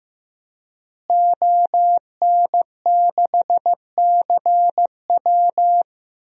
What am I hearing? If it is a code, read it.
Morse code, heard as a keyed tone, about 15 wpm, 700 Hz. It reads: ON6CW
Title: DXCC-100 Call Signs - 5 Characters - Repeated 3x - 15wpm